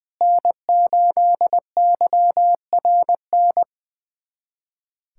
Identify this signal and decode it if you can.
Morse code, heard as a keyed tone, about 20 wpm, 700 Hz. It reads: N8YRN
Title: USA Callsign DE Callsign - 5 Characters - 20wpm